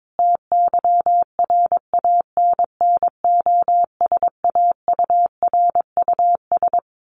Morse code, heard as a keyed tone, about 22 wpm, 700 Hz. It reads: TYRANNOSAURUS
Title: A Sound Of Thunder 22wpm